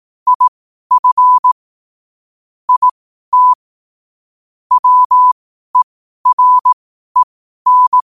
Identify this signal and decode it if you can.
Morse code, heard as a keyed tone, about 18 wpm, spaced as for 12 wpm, 1000 Hz. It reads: IF IT WEREN
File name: qotdfiles / 12wpm